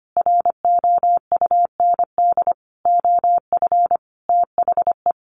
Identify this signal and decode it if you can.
Morse code, heard as a keyed tone, about 25 wpm, 700 Hz. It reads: ROUND OF THE
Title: Sentences from Top 700 Words 25wpm